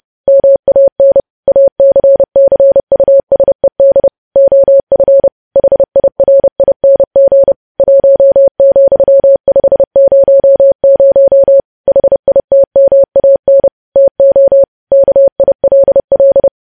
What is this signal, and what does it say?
Morse code, heard as a keyed tone, about 30 wpm, 550 Hz. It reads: MAN ACCUSED OF HIRING 1,500 HITMAN TO KILL
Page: News Headlines in Morse Code at 30 WPM